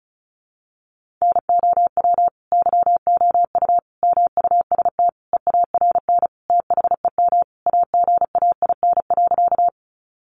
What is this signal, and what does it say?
Morse code, heard as a keyed tone, about 34 wpm, 700 Hz. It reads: NOW YOU MUST EARN THEM AGAIN.
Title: Sentences from Top 1500 Words 35wpm